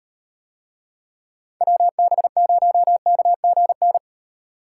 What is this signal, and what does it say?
Morse code, heard as a keyed tone, about 36 wpm, 700 Hz. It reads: WB0KGN